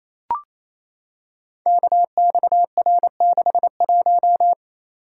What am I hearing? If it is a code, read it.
Morse code, heard as a keyed tone, about 28 wpm, 700 Hz. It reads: KXR61
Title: Code Groups - 5 Letter:Numbers - 28wpm